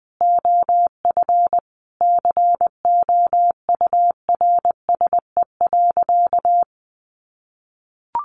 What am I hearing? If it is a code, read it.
Morse code, heard as a keyed tone, about 20 wpm, 700 Hz. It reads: OF COURSE.
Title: Sentences from Top 1500 Words 20wpm